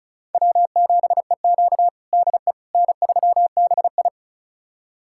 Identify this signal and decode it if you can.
Morse code, heard as a keyed tone, about 34 wpm, 700 Hz. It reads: W7EQ DE N3BI
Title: USA Callsign DE Callsign - 4 Characters - 35wpm